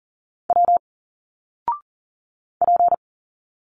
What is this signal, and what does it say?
Morse code, heard as a keyed tone, about 40 wpm, 700 Hz. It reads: W P
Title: Single Letters 40wpm